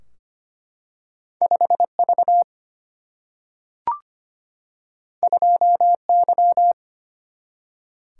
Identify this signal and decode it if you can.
Morse code, heard as a keyed tone, about 25 wpm, 700 Hz. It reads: 5 V 2 Y
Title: Single-Letter Single-Number 25wpm